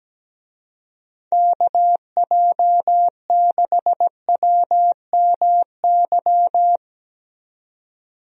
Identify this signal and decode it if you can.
Morse code, heard as a keyed tone, about 17 wpm, 700 Hz. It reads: KJ6WMY
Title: USA Call Signs - 6 Characters - 17wpm